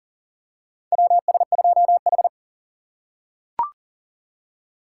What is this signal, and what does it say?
Morse code, heard as a keyed tone, about 40 wpm, 700 Hz. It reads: WS2H
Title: USA Call Signs - 4 Characters - 6x Character Spacing - 40wpm